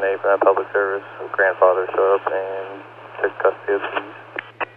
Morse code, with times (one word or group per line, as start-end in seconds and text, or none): none